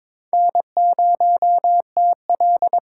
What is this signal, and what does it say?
Morse code, heard as a keyed tone, about 21 wpm, 700 Hz. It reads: N0TL
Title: USA Call Signs - 4 Characters - 4x Character Spacing - 22wpm